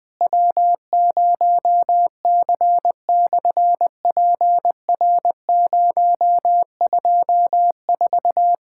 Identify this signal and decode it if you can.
Morse code, heard as a keyed tone, about 20 wpm, 700 Hz. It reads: W0C/PR024
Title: SOTA QSO Practice 20wpm